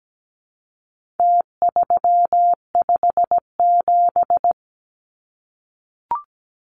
Morse code, heard as a keyed tone, about 17 wpm, 700 Hz. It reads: T357
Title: Numbers - Contest Style - 4-Digit 17wpm